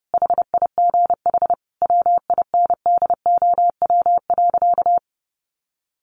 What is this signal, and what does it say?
Morse code, heard as a keyed tone, about 30 wpm, 700 Hz. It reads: HIGH WINDOW.
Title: A Sound Of Thunder 30wpm